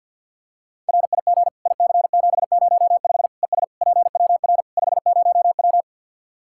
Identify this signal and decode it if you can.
Morse code, heard as a keyed tone, about 47 wpm, 700 Hz. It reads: RIG IC705 ES PWR 50W
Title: Practice QSO - Exchange 2 - Info - 50wpm